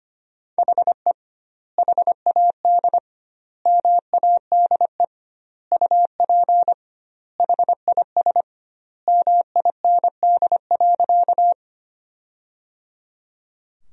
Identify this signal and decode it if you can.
Morse code, heard as a keyed tone, about 25 wpm, 700 Hz. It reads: HE HAD MADE UP HIS MIND.
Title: 2x Word-Spacing - Sentences from Top 500 Words - 25wpm